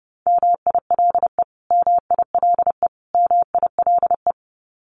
Morse code, heard as a keyed tone, about 30 wpm, 700 Hz. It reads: MILE MILE MILE
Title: Top 500 Words Repeated 30wpm